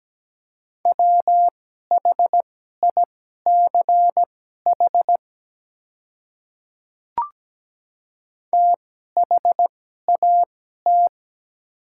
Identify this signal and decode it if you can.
Morse code, heard as a keyed tone, about 17 wpm, spaced as for 12 wpm, 700 Hz. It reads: WHICH THAT
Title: 2x Character Spacing - Top 200 Words - 17wpm